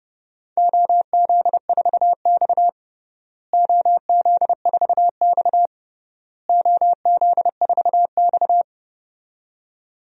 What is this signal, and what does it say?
Morse code, heard as a keyed tone, about 32 wpm, 700 Hz. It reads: OZ4X OZ4X OZ4X
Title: DXCC-100 Call Signs - 4 Characters - Repeated 3x - 30wpm